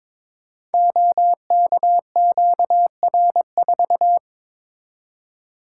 Morse code, heard as a keyed tone, about 22 wpm, 690 Hz. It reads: OKQR4
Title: Code Groups - 5 Letter:Numbers - 22wpm